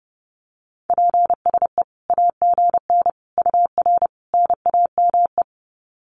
Morse code, heard as a keyed tone, about 28 wpm, 700 Hz. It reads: PSE AGN UR NAME
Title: Common QSO Phrases 30wpm